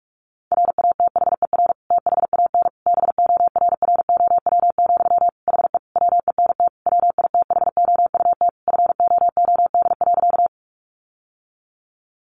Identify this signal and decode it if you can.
Morse code, heard as a keyed tone, about 44 wpm, 700 Hz. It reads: RATHER THAN BORROW, HE WENT WITHOUT FOOD.
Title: Sentences from Top 1500 Words 45wpm